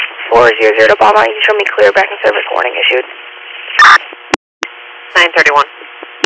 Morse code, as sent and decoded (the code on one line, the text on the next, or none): none